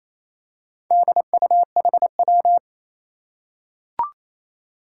Morse code, heard as a keyed tone, about 27 wpm, 700 Hz. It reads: DUHW
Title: Four Challenging Letter-Number Combinations 28wpm